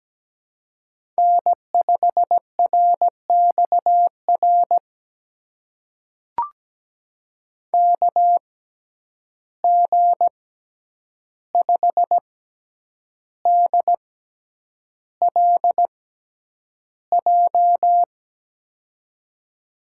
Morse code, heard as a keyed tone, about 17 wpm, spaced as for 8 wpm, 700 Hz. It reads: N5RXR KG5DLJ
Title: USA Call Signs - 6x Character Spacing - 17wpm